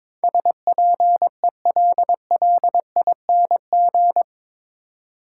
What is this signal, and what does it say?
Morse code, heard as a keyed tone, about 21 wpm, 700 Hz. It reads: SPELLING
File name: practice / 22wpm / Eight Letter Words - 22wpm